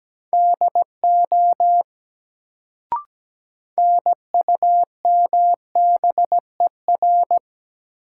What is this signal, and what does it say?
Morse code, heard as a keyed tone, about 17 wpm, 700 Hz. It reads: DO NUMBER
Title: Sentences from Top 100 Words - New Word Drill - 17wpm